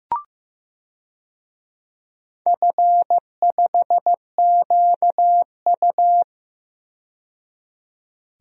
Morse code, heard as a keyed tone, about 15 wpm, 700 Hz. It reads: F5QU
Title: DXCC-100 Call Signs - 4 Characters - 15wpm